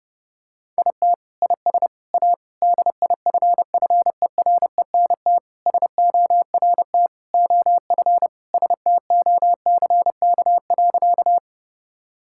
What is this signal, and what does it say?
Morse code, heard as a keyed tone, about 30 wpm, 700 Hz. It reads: IT IS A DIFFERENT SORT OF STOCK.
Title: Sentences from Top 1000 Words - Review Part 5 - 30wpm